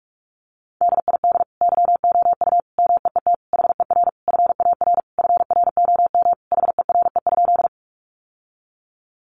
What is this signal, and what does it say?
Morse code, heard as a keyed tone, about 44 wpm, 700 Hz. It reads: DID YOU MEET HER FAR FROM HERE?
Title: Sentences from Top 1100 Words 45wpm